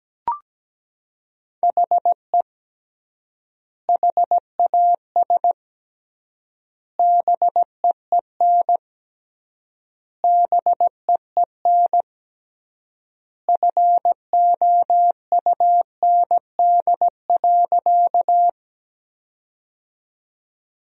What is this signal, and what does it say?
Morse code, heard as a keyed tone, about 17 wpm, 700 Hz. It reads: HE HAS BEEN BEEN FOUND.
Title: 3x Word-Spacing - Sentences from Top 300 Words - 17wpm